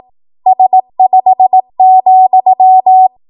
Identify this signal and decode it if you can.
Morse code, heard as a keyed tone, about 18 wpm, 750 Hz. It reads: S5,
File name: Morse / 18WPM